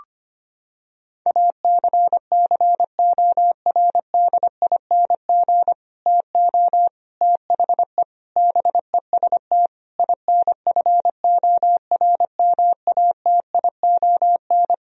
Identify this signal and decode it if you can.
Morse code, heard as a keyed tone, about 25 wpm, 700 Hz. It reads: ACCORDING TO THE BEST INFORMATION
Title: Sentences from Top 1000 Words - Review Part 4 - 25wpm